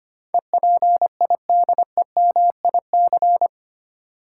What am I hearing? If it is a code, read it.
Morse code, heard as a keyed tone, about 25 wpm, 700 Hz. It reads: EPIDEMIC
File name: 8 Letter Words - First 3 Letters Given - 25wpm